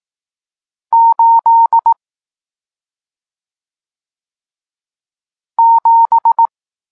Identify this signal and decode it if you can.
Morse code, heard as a keyed tone, about 18 wpm, 920 Hz. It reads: 8 7
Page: Tekst 40 – Nauka znaków przestankowych